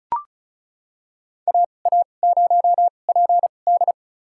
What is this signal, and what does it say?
Morse code, heard as a keyed tone, about 36 wpm, spaced as for 25 wpm, 700 Hz. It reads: AA0PD
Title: USA Call Signs - 5 Characters - 2x Character Spacing - 35wpm